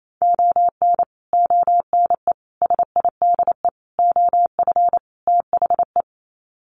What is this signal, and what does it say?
Morse code, heard as a keyed tone, about 28 wpm, 700 Hz. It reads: ON ONE SIDE OF THE
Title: Sets of 5 Words 28wpm